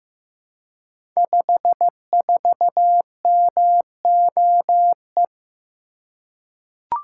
Morse code, heard as a keyed tone, about 15 wpm, 700 Hz. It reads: 54MOE
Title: Code Groups - 5 Letter:Numbers - 15wpm